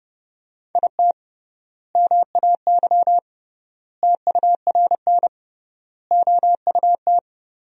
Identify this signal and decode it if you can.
Morse code, heard as a keyed tone, about 29 wpm, 700 Hz. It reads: IT MAY TURN OUT
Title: 3x Word-Spacing - Sentences from Top 400 Words - 30wpm